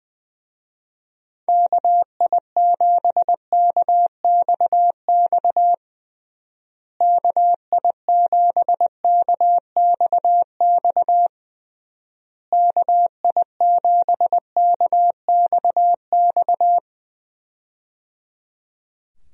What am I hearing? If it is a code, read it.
Morse code, heard as a keyed tone, about 20 wpm, 700 Hz. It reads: KI7KXX KI7KXX KI7KXX
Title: USA Call Signs - 6 Characters - Repeated 3x - 20wpm